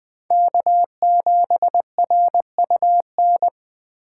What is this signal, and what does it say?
Morse code, heard as a keyed tone, about 20 wpm, 700 Hz. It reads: K7RUN
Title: Speed Racing - USA 1x3 Call Signs 17wpm